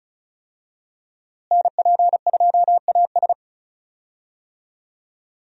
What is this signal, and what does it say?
Morse code, heard as a keyed tone, about 36 wpm, 700 Hz. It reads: NP2AS
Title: USA Call Signs - 5 Characters - Repeated 3x - 35wpm